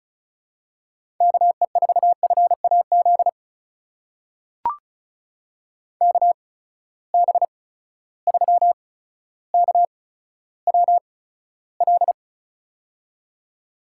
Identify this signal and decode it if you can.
Morse code, heard as a keyed tone, about 34 wpm, spaced as for 15 wpm, 700 Hz. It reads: KE4FAZ KB3KWL
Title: USA Call Signs - 6 Characters - 8x Character Spacing - 35wpm